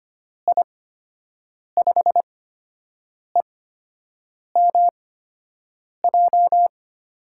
Morse code, heard as a keyed tone, about 25 wpm, spaced as for 7 wpm, 700 Hz. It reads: I5EMJ
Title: USA Call Signs - 6 Characters - 8x Character Spacing - 25wpm